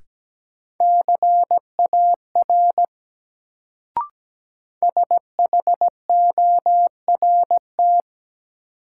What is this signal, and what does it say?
Morse code, heard as a keyed tone, about 17 wpm, 700 Hz. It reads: CAR SHORT
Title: Sets of 5 Words Encore - New Word Drill - 17wpm